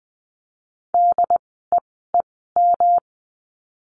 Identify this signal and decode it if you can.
Morse code, heard as a keyed tone, about 20 wpm, spaced as for 15 wpm, 700 Hz. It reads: DEEM
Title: 108 Lesson 18f - Letter m - Words 20wpm